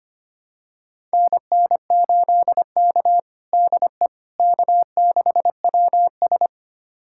Morse code, heard as a keyed tone, about 25 wpm, 700 Hz. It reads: NN8K DE K6WS
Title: USA Callsign DE Callsign - 4 Characters - 25wpm